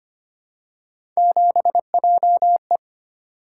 Morse code, heard as a keyed tone, about 25 wpm, 700 Hz. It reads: 7JE
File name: Three Letter-Number Combinations - 25wpm